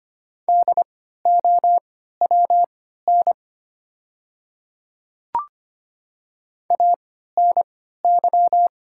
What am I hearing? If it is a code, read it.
Morse code, heard as a keyed tone, about 25 wpm, spaced as for 14 wpm, 690 Hz. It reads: DOWN ANY
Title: 3x Character Spacing - Top 200 Words - 25wpm